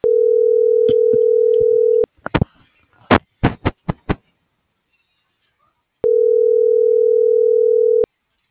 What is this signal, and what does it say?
Ringback tone (North America), 440 + 480 Hz.
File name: CENTRAL TERRESTRE / out vane.lazaro - .34762